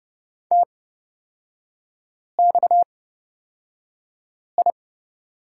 Morse code, heard as a keyed tone, about 31 wpm, spaced as for 5 wpm, 690 Hz. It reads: TXI
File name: Lesson 37e - Letter x - Three Characters 30wpm